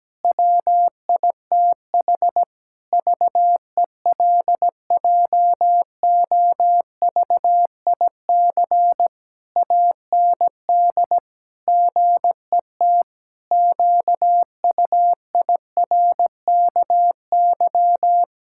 Morse code, heard as a keyed tone, about 17 wpm, 700 Hz. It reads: WITH VELJOVIC AND GET QUIRKY
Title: Pangram Challenge v2 - 17wpm